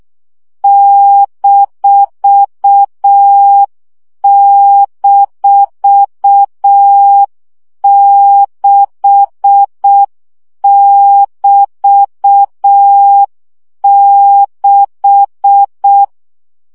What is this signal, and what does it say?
Morse code, heard as a keyed tone, about 6 wpm, 800 Hz. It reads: --6<BT>6
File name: CW gyakorlas b 6 egyenlo minusz 5 baud